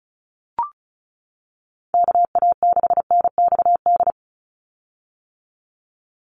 Morse code, heard as a keyed tone, about 34 wpm, 700 Hz. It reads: KA6NXD
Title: USA Call Signs 35wpm